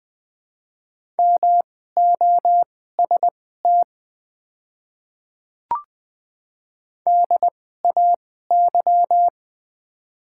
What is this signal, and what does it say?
Morse code, heard as a keyed tone, about 20 wpm, spaced as for 14 wpm, 700 Hz. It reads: MOST DAY
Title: 2x Character Spacing - Top 100 Words - 20wpm